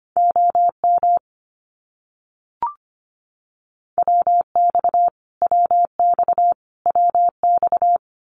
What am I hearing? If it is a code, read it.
Morse code, heard as a keyed tone, about 25 wpm, 700 Hz. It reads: OM WX WX WX
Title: Abbreviations Repeated 3x - 25wpm